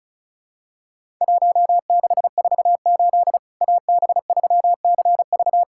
Morse code, heard as a keyed tone, about 36 wpm, 700 Hz. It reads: 1648 AB3CV
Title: Worked All Europe DX Contest - QTC Exchange - 35wpm